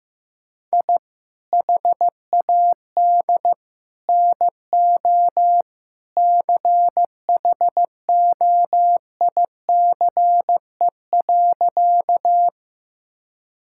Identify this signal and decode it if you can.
Morse code, heard as a keyed tone, about 15 wpm, 700 Hz. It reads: I HAD NO CHOICE.
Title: Sentences from Top 1000 Words - Review Part 6 - 15wpm